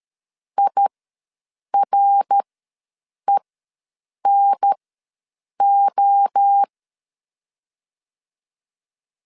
Morse code, heard as a keyed tone, about 13 wpm, spaced as for 7 wpm, 800 Hz. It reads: IRENO
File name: LEZIONI / 5 LBVC / 5 LBV ws 12wpm 3015